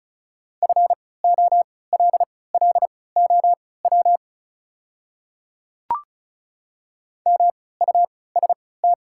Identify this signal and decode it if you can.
Morse code, heard as a keyed tone, about 36 wpm, spaced as for 20 wpm, 700 Hz. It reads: FOLLOW MUST
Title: 3x Character Spacing - Top 200 Words - 35wpm